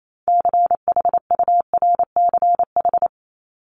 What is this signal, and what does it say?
Morse code, heard as a keyed tone, about 28 wpm, 700 Hz. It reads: CHURCH
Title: Top 1751-2000 Words 28wpm